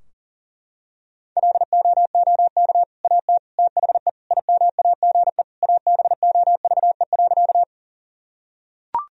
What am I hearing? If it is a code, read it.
Morse code, heard as a keyed tone, about 37 wpm, 700 Hz. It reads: LOOK AT THE IMAGE ABOVE.
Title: Sentences from Top 1000 Words - Review Part 4 - 40wpm